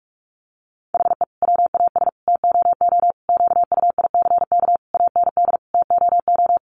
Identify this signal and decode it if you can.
Morse code, heard as a keyed tone, about 44 wpm, 700 Hz. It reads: HE WAS TOO QUICK AND TOO
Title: Sentences from Top 600 Words 45wpm